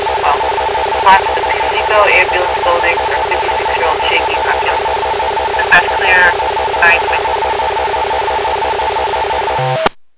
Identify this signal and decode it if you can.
Two-tone teleprinter signal, about 450 Hz shift.
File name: Piseco EMS 2023 08 28 09 22 13